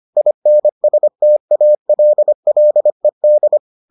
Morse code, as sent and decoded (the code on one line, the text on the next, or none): .. -. ... - .- .-.. .-.. -..
INSTALLED